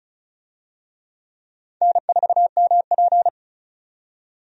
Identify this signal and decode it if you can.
Morse code, heard as a keyed tone, about 34 wpm, 700 Hz. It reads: N4MP
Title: USA Call Signs - 4 Characters - 35wpm